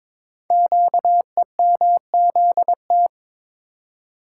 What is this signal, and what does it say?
Morse code, heard as a keyed tone, about 22 wpm, 700 Hz. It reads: QEMZT
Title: Code Groups - 5 Letters - 22wpm